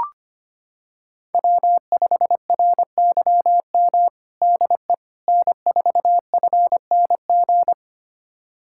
Morse code, heard as a keyed tone, about 25 wpm, 700 Hz. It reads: W5RYM DE N4FNG
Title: USA Callsign DE Callsign - 5 Characters - 25wpm